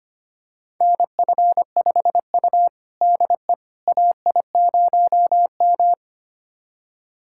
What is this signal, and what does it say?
Morse code, heard as a keyed tone, about 25 wpm, 700 Hz. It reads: NF5U DE AI0M
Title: USA Callsign DE Callsign - 4 Characters - 25wpm